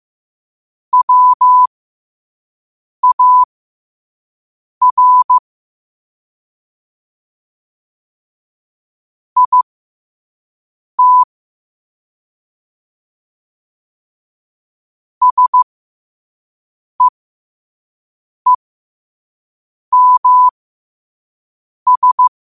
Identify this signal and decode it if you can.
Morse code, heard as a keyed tone, about 15 wpm, spaced as for 5 wpm, 1000 Hz. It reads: WAR IT SEEMS
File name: qotdfiles / 5wpm